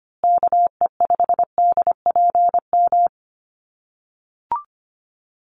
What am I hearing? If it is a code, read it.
Morse code, heard as a keyed tone, about 25 wpm, 700 Hz. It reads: KE5DPM
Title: USA Call Signs Repeated 2x 25wpm